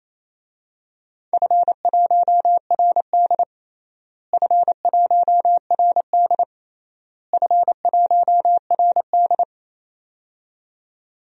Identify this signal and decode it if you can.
Morse code, heard as a keyed tone, about 28 wpm, 700 Hz. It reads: F1RD F1RD F1RD
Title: DXCC-100 Call Signs - 4 Characters - Repeated 3x - 28wpm